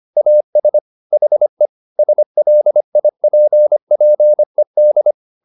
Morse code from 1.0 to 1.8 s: HE

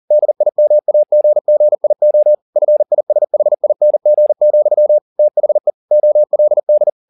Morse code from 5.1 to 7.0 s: THE OLD

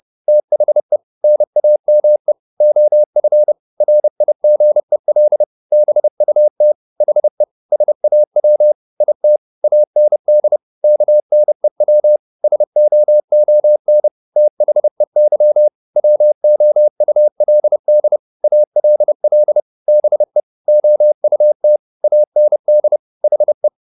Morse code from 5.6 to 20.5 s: BUT HE SAW IT AND KNEW SOON THEY WOULD ALL BE

